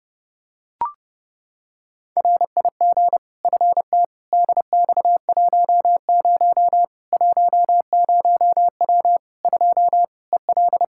Morse code, heard as a keyed tone, about 30 wpm, 700 Hz. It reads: RIG FTDX10 10W 2 EL
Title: Practice QSO - Exchange 2 - Info - 30wpm